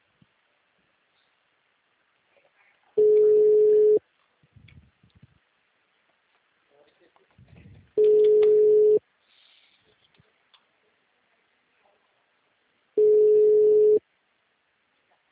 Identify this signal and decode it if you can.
Ringback tone (Europe, 425 Hz), with speech.
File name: out- -2016- - - .69